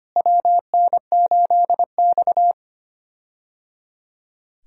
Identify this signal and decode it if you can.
Morse code, heard as a keyed tone, about 25 wpm, 700 Hz. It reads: WN8X
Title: USA Call Signs - 4 Characters - Repeated 3x - 25wpm